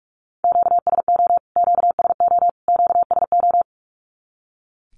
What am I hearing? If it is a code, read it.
Morse code, heard as a keyed tone, about 44 wpm, 700 Hz. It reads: QSO QSO QSO
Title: Q-Codes Repeated 3x 45wpm